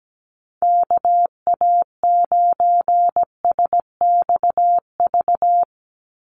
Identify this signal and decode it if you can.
Morse code, heard as a keyed tone, about 17 wpm, 700 Hz. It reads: KA9SXV
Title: Call Signs - 17wpm